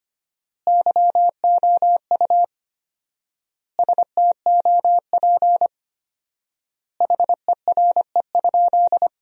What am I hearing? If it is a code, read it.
Morse code, heard as a keyed tone, about 25 wpm, 700 Hz. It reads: YOU STOP HERE?
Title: 4x Word-Spacing - Sentences from Top 300 Words - 25wpm